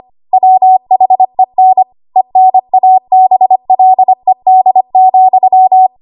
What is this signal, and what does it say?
Morse code, heard as a keyed tone, about 25 wpm, 750 Hz. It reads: WHEN ENABLED,
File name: Archive / 25 WPM / 25WPM